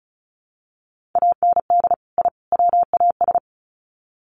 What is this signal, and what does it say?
Morse code, heard as a keyed tone, about 34 wpm, 700 Hz. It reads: AND I WAS
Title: Sets of 3 Words 35wpm